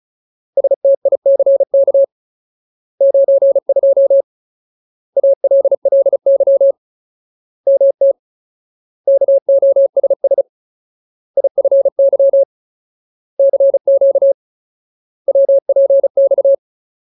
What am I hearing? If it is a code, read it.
Morse code, heard as a keyed tone, about 34 wpm, 550 Hz. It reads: STICK 92 ALLY MT KOSS IFY CQ WPX
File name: Ultimate Word List / 35WPM Ultimate